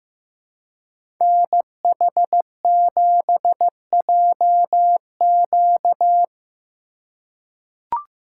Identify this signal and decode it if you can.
Morse code, heard as a keyed tone, about 15 wpm, 700 Hz. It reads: NH7JQ
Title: USA Call Signs - 5 Characters - 4x Character Spacing - 15wpm